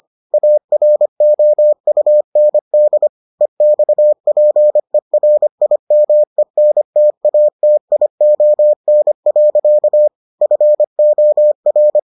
Morse code, heard as a keyed tone, about 25 wpm, 600 Hz. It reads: AROUND EXPERIMENTATION. FOR